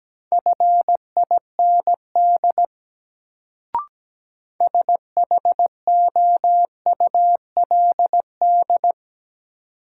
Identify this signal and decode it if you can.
Morse code, heard as a keyed tone, about 17 wpm, 700 Hz. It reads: FIND SHOULD
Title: Sets of 2 Words - New Word Drill - 17wpm